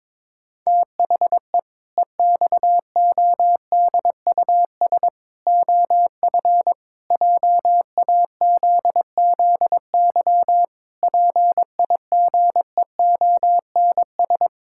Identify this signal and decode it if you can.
Morse code, heard as a keyed tone, about 22 wpm, 700 Hz. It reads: THE EXODUS OF JAZZY PIGEONS
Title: Pangram Challenge v2 - 22wpm